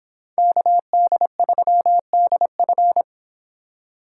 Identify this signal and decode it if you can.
Morse code, heard as a keyed tone, about 26 wpm, 700 Hz. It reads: KD3DF